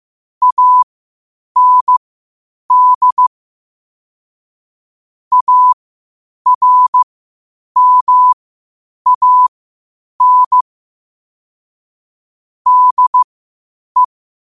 Morse code, heard as a keyed tone, about 15 wpm, spaced as for 8 wpm, 1000 Hz. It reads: AND ARMAN DE
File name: qotdfiles / 8wpm